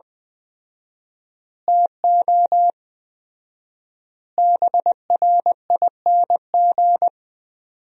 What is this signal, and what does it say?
Morse code, heard as a keyed tone, about 20 wpm, 700 Hz. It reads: TO BRING
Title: 4x Word-Spacing - Sentences from Top 500 Words - 20wpm